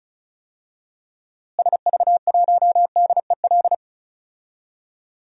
Morse code, heard as a keyed tone, about 36 wpm, 700 Hz. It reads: SV1DEL
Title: DXCC-100 Call Signs - 6 Characters - Repeated 2x - 35wpm